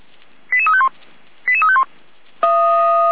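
Five-tone selective call (ZVEI), heard as quick runs of five tones, tones of about 70 ms. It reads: 88351 88351
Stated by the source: ZVEI five-tone selective call received by SDR.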